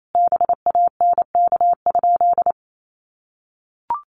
Morse code, heard as a keyed tone, about 28 wpm, 700 Hz. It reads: BANK?
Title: Sentences from Top 1200 Words 28wpm